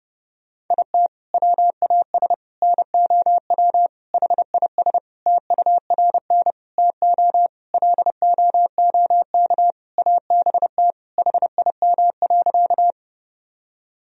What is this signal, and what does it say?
Morse code, heard as a keyed tone, about 30 wpm, 700 Hz. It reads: IT WAS NOW HIS TURN TO LOOK ABT HIM.
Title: Sentences from Top 200 Words and Abbreviations 30wpm